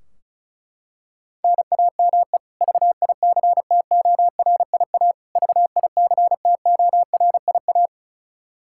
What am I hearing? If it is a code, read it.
Morse code, heard as a keyed tone, about 36 wpm, 700 Hz. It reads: NAME VICTORIA VICTORIA
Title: K1USN SST QSO Practice 35wpm